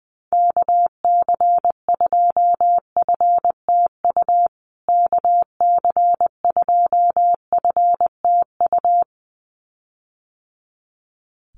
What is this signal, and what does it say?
Morse code, heard as a keyed tone, about 20 wpm, 700 Hz. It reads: KC2FTU KC2FTU
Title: USA Call Signs Repeated 2x 20wpm